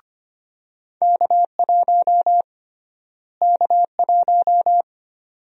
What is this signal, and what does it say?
Morse code, heard as a keyed tone, about 25 wpm, 700 Hz. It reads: K1 K1
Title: USA Call Sign Prefixes - Repeated 2x - 25wpm